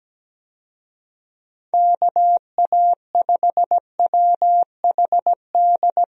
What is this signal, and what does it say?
Morse code, heard as a keyed tone, about 17 wpm, 700 Hz. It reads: KA5WHD
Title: USA Call Signs - 6 Characters - 17wpm